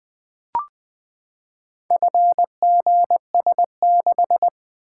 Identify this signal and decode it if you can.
Morse code, heard as a keyed tone, about 20 wpm, 700 Hz. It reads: FGS6
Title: Four Letter-Number Combinations - 20wpm